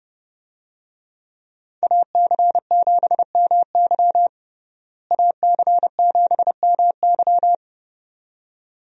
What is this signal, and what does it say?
Morse code, heard as a keyed tone, about 31 wpm, 700 Hz. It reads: AC7MY AC7MY
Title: USA Call Signs - 5 Characters - Repeated 2x - 30wpm